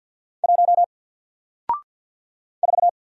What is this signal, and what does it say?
Morse code, heard as a keyed tone, about 54 wpm, 700 Hz. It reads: 1 4
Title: Single Letter-Number Alt 50wpm